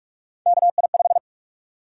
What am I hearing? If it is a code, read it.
Morse code, heard as a keyed tone, about 46 wpm, 700 Hz. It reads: KI5
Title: USA Call Sign Prefixes - Repeated 2x - 45wpm